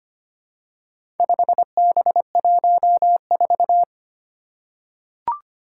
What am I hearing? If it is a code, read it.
Morse code, heard as a keyed tone, about 25 wpm, 700 Hz. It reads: 5B14
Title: Four Challenging Letter-Number Combinations 25wpm